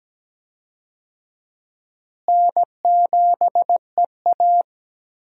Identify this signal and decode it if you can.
Morse code, heard as a keyed tone, about 17 wpm, 700 Hz. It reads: N7EA
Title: USA Call Signs - 4 Characters - Repeated 2x - 17wpm